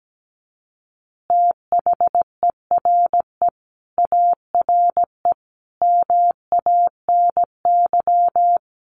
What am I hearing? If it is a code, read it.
Morse code, heard as a keyed tone, about 17 wpm, 700 Hz. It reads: THERE ARE MANY
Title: Easy Phrases 17wpm